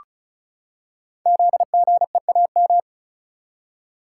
Morse code, heard as a keyed tone, about 36 wpm, 700 Hz. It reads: ZGEAM
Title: Code Groups - 5 Letters - 35wpm